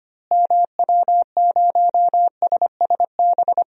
Morse code, heard as a keyed tone, about 25 wpm, 700 Hz. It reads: MW0SSB
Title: DXCC-100 Call Signs - 6 Characters - Repeated 3x - 25wpm